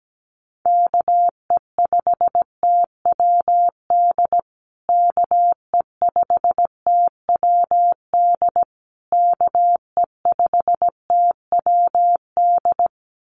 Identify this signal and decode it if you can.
Morse code, heard as a keyed tone, about 17 wpm, 700 Hz. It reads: KE5TWD KE5TWD KE5TWD
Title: USA Call Signs Repeated 3x 17wpm